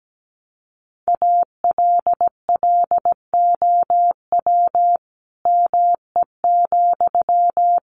Morse code, heard as a keyed tone, about 17 wpm, 700 Hz. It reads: ALLOW ME,
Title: Sentences from Top 700 Words 17wpm